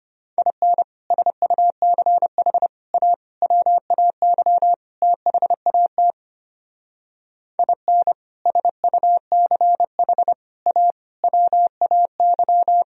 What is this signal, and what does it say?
Morse code, heard as a keyed tone, about 32 wpm, 700 Hz. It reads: IN SUCH A WAY THAT IN SUCH A WAY